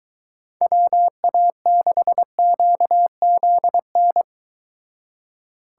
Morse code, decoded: WA6QZN

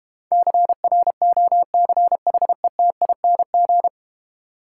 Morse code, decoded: CROCHETING

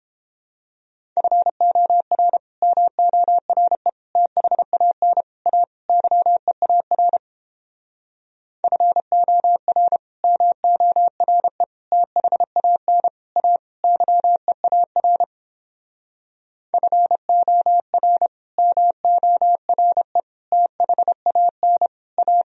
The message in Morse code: FOR MORE THAN A YEAR FOR MORE THAN A YEAR FOR MORE THAN A

